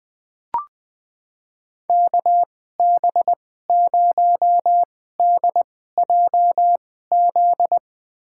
Morse code, heard as a keyed tone, about 21 wpm, spaced as for 15 wpm, 700 Hz. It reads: KB0DJZ
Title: USA Call Signs - 6 Characters - 2x Character Spacing - 20wpm